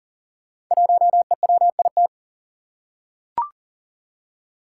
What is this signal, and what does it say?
Morse code, heard as a keyed tone, about 39 wpm, 700 Hz. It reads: 1EWIT
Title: Five Challenging Letter-Number Combinations 40wpm